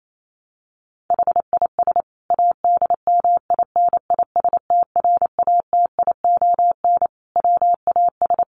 Morse code, heard as a keyed tone, about 28 wpm, 700 Hz. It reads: HIS ADMINISTRATION WAS
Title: Sentences from Top 1300 Words 28wpm